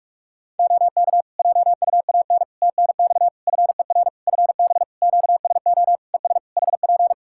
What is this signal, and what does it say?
Morse code, heard as a keyed tone, about 42 wpm, 700 Hz. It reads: OK JUAN TNX FER FB QSO ES HP
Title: K1USN SST QSO Practice 45wpm